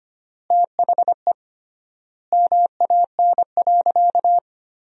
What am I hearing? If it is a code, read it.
Morse code, heard as a keyed tone, about 25 wpm, 700 Hz. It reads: THE MAN.
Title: 3x Word-Spacing - Sentences from Top 200 Words - 25wpm